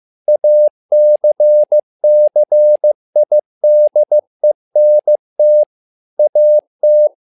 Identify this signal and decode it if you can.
Morse code, heard as a keyed tone, about 15 wpm, 600 Hz. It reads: ACCIDENT AT
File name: learning center / tower trouble / tower trouble 15wpm